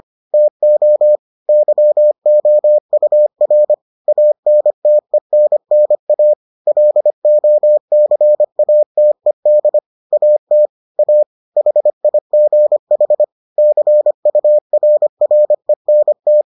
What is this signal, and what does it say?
Morse code, heard as a keyed tone, about 25 wpm, 600 Hz. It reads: TO YOUR ANTENNA LOCATED AT A HIGH CURRENT